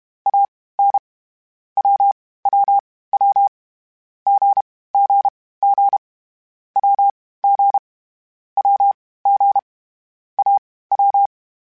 Morse code, heard as a keyed tone, about 32 wpm, spaced as for 18 wpm, 800 Hz. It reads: AN WWW GGG WG WG AW
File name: Short Runs / Short Runs-mp3s / Unballanced @30